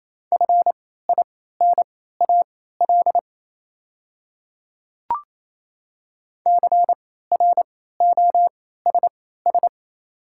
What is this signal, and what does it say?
Morse code, heard as a keyed tone, about 28 wpm, spaced as for 16 wpm, 700 Hz. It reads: FINAL CROSS
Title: 3x Character Spacing - Top 500 Words - 28wpm